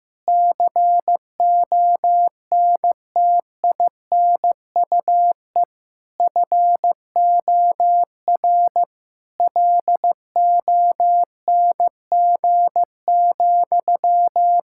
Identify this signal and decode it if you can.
Morse code, heard as a keyed tone, about 14 wpm, 700 Hz. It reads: CONTINUE FOR LONG,
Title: Sentences from Top 1000 Words - Review Part 6 - 15wpm